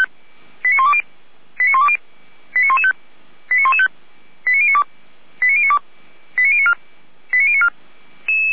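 Five-tone selective call (ZVEI), heard as quick runs of five tones, tones of about 70 ms. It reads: 89120 89120 89115 89115 89092 89092 89094 89094
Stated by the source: ZVEI five-tone selective call 89020 received on SDR.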